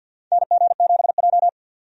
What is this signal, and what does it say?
Morse code, heard as a keyed tone, about 52 wpm, 700 Hz. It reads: NG7J
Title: USA Call Signs - 4 Characters - 50wpm